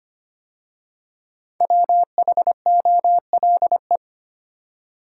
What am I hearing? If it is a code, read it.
Morse code, heard as a keyed tone, about 25 wpm, 700 Hz. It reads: WHOLE
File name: 5 Letter Words - First 3 Letters Given - 25wpm